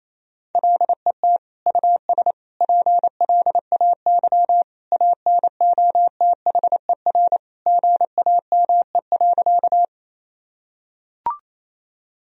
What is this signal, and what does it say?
Morse code, heard as a keyed tone, about 27 wpm, 700 Hz. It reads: LET US PLAY ANOTHER GAME.
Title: Sentences from Top 1000 Words - Review Part 3 - 28wpm